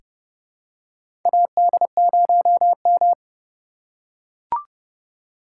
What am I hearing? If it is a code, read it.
Morse code, heard as a keyed tone, about 31 wpm, 700 Hz. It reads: AD0M
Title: USA Call Signs - 3x Character Spacing - 30wpm